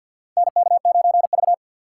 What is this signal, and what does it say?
Morse code, heard as a keyed tone, about 53 wpm, 700 Hz. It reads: NK9V
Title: USA Call Signs - 4 Characters - 6x Character Spacing - 50wpm